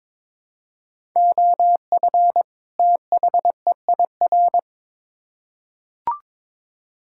Morse code, heard as a keyed tone, about 22 wpm, 700 Hz. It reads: OF THEIR
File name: Build A Phrase - 2-Word Phrases - 22wpm